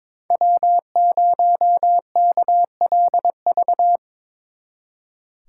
Morse code, heard as a keyed tone, about 25 wpm, 700 Hz. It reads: W0KLV
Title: Speed Racing - USA 1x3 Call Signs 22wpm